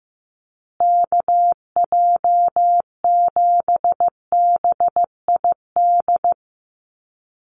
Morse code, heard as a keyed tone, about 15 wpm, 700 Hz. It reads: KJ7BID
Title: USA Call Signs Repeated 3x 15wpm